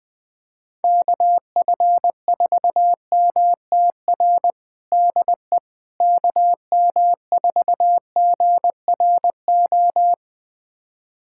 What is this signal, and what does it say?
Morse code, heard as a keyed tone, about 20 wpm, 700 Hz. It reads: KF4MTR DE KM4GRO
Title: USA Callsign DE Callsign 20wpm